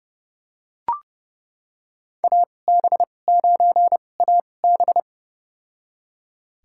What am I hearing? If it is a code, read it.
Morse code, heard as a keyed tone, about 30 wpm, spaced as for 22 wpm, 700 Hz. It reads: AB9AB
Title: USA Call Signs - 5 Characters - 2x Character Spacing - 30wpm